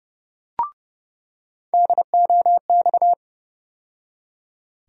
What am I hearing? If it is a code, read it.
Morse code, heard as a keyed tone, about 30 wpm, 700 Hz. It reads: DOX
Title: USA Call Sign Postfixes - 3 Letters 30wpm